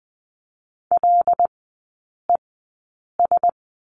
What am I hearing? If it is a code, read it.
Morse code, heard as a keyed tone, about 20 wpm, spaced as for 8 wpm, 700 Hz. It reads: LES
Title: 074 Lesson 13e - Letter L - Three Characters 20wpm